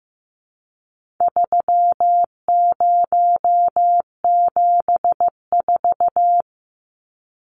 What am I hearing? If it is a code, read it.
Morse code, heard as a keyed tone, about 15 wpm, 700 Hz. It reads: 3074
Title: Numbers - 4-Digit 15wpm